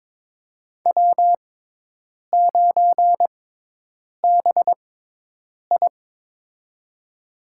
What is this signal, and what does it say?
Morse code, heard as a keyed tone, about 22 wpm, spaced as for 8 wpm, 700 Hz. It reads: W9BI
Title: USA Call Signs - 6x Character Spacing - 22wpm